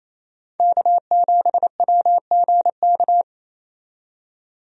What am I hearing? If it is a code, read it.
Morse code, heard as a keyed tone, about 28 wpm, 700 Hz. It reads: K7WGK
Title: USA Call Signs - 5 Characters - 28wpm